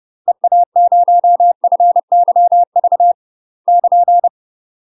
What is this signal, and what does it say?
Morse code, heard as a keyed tone, about 30 wpm, 700 Hz. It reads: EA0FYV <KN>
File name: UK MASTER QSOs TRAINING / 30WPM / UK MASTER QSOs 0003